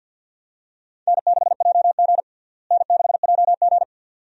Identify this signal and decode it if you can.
Morse code, heard as a keyed tone, about 51 wpm, 700 Hz. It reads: N6JG N6JG
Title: USA Call Signs - 4 Characters - Repeated 2x - 50wpm